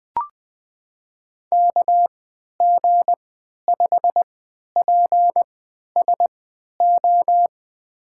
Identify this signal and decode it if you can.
Morse code, heard as a keyed tone, about 20 wpm, spaced as for 11 wpm, 700 Hz. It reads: KG5PSO
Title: USA Call Signs - 3x Character Spacing - 20wpm